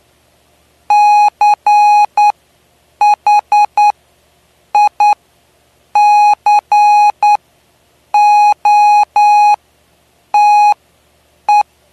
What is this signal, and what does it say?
Morse code, heard as a keyed tone, about 9 wpm, spaced as for 7 wpm, 830 Hz. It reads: CHICOTE
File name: cw / py2lcd / 09-PY2LCD-Li��o 9